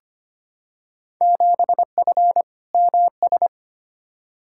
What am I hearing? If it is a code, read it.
Morse code, heard as a keyed tone, about 25 wpm, 700 Hz. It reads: 7F MS
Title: Field Day Exchanges Encore - 25wpm